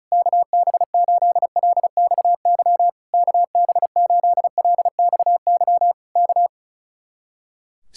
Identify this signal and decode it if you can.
Morse code, heard as a keyed tone, about 33 wpm, 700 Hz. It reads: KB8LXY KB8LXY K
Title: Speed-Racing Common QSO Phrases Encore - 35wpm